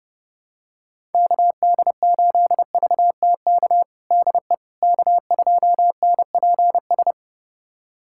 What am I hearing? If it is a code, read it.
Morse code, heard as a keyed tone, about 30 wpm, 700 Hz. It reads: KD8VTK DE K2NPS
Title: USA Callsign DE Callsign 30wpm